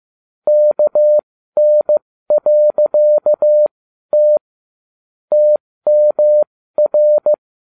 Morse code, heard as a keyed tone, about 15 wpm, spaced as for 12 wpm, 600 Hz. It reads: KN.T TMR